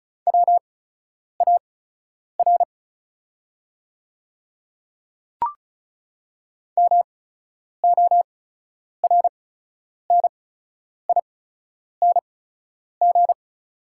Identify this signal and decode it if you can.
Morse code, heard as a keyed tone, about 36 wpm, spaced as for 10 wpm, 690 Hz. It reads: WAR MORNING
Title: 8x Character Spacing - Top 500 Words - 35wpm